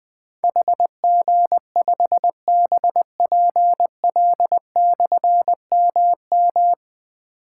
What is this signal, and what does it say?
Morse code, heard as a keyed tone, about 20 wpm, 700 Hz. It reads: HG5BPL/MM
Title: Difficult Call Signs 20wpm